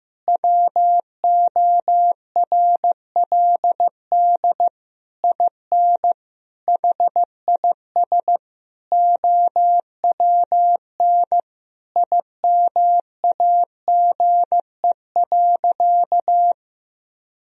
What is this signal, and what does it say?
Morse code, heard as a keyed tone, about 16 wpm, 700 Hz. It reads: WORLD IN HIS OWN IMAGE.
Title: Sentences from Top 1000 Words - Review Part 4 - 15wpm